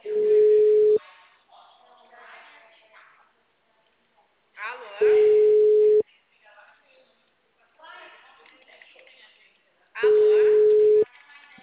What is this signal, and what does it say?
Ringback tone (Europe, 425 Hz), with speech.